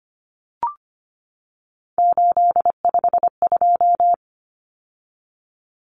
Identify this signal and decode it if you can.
Morse code, heard as a keyed tone, about 25 wpm, 700 Hz. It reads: 852
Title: Common Cut Numbers - 3-Digit 25wpm